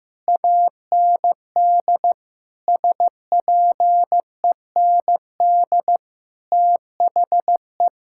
Morse code, heard as a keyed tone, about 14 wpm, 700 Hz. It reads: AND SPEND THE
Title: Sentences from Top 1000 Words - Review Part 5 - 15wpm